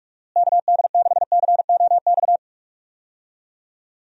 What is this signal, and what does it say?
Morse code, heard as a keyed tone, about 42 wpm, 700 Hz. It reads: KD6COX
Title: USA Call Signs - 6 Characters - Repeated 2x - 45wpm